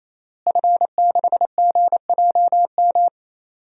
Morse code, heard as a keyed tone, about 27 wpm, 700 Hz. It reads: F6GJM
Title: DXCC-100 Call Signs - 5 Characters - Repeated 3x - 28wpm